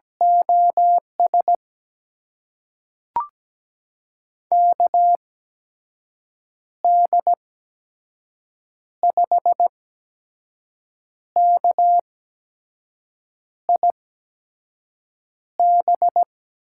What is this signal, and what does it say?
Morse code, heard as a keyed tone, about 17 wpm, spaced as for 5 wpm, 700 Hz. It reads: OS KD5KIB
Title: USA Call Signs - 6 Characters - 8x Character Spacing - 17wpm